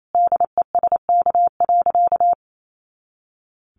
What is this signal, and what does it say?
Morse code, heard as a keyed tone, about 28 wpm, 700 Hz. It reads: DESK.
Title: A Sound Of Thunder 28wpm